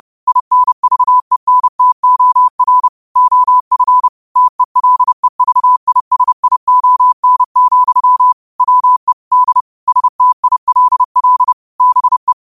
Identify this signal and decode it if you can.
Morse code, heard as a keyed tone, about 30 wpm, 1000 Hz. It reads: INVENTOR OF TELEVISION, WE D STILL BE